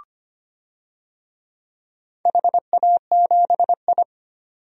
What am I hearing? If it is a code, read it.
Morse code, heard as a keyed tone, about 25 wpm, 700 Hz. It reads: HA7I
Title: DXCC-100 Call Signs - 4 Characters - 25wpm